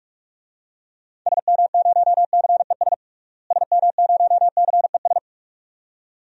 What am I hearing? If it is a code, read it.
Morse code, heard as a keyed tone, about 46 wpm, 700 Hz. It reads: SM0CES SM0CES
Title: DXCC-100 Call Signs - 6 Characters - Repeated 2x - 45wpm